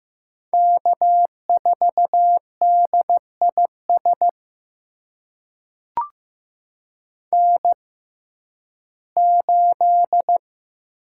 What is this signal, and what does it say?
Morse code, heard as a keyed tone, about 15 wpm, spaced as for 11 wpm, 700 Hz. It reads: K4DIS N8
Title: USA Call Signs - 5 Characters - 6x Character Spacing - 15wpm